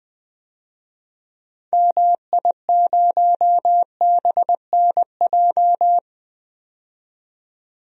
Morse code, heard as a keyed tone, about 20 wpm, 700 Hz. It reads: MI0BNJ
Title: DXCC-100 Call Signs - 6 Characters - 20wpm